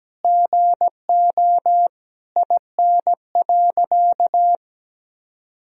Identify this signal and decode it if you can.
Morse code, heard as a keyed tone, about 17 wpm, 700 Hz. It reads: GO IN.
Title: Sentences from Top 400 Words and Abbreviations 17wpm